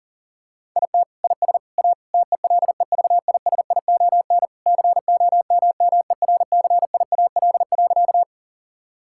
Morse code, heard as a keyed tone, about 40 wpm, 700 Hz. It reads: IT IS A TELEVISION COMMERCIAL.
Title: Sentences from Top 1000 Words - Review Part 5 - 40wpm